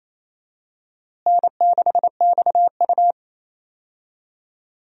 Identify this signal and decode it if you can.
Morse code, heard as a keyed tone, about 28 wpm, 700 Hz. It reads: N6XU
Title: USA Call Signs - 4 Characters - Repeated 3x - 28wpm